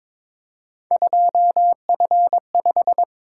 Morse code, heard as a keyed tone, about 26 wpm, 700 Hz. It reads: 2F5
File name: Speed-Racing - Three Letter-Number Combinations - 22wpm